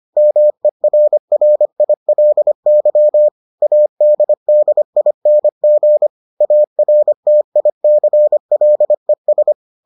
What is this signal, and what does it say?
Morse code, heard as a keyed tone, about 25 wpm, 600 Hz. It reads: MERRILY ADDING ARTICLES